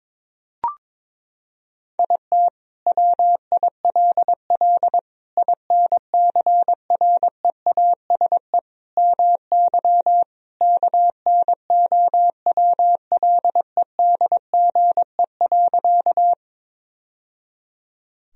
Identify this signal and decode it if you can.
Morse code, heard as a keyed tone, about 22 wpm, 700 Hz. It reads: IT WILL INCREASE MY KNOWLEDGE.
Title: Sentences from Top 1000 Words - Review Part 4 - 22wpm